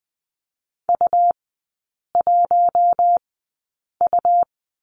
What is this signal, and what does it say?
Morse code, heard as a keyed tone, about 20 wpm, spaced as for 8 wpm, 700 Hz. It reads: U1U
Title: 100 Lesson 17e - Letter u - Three Characters 20wpm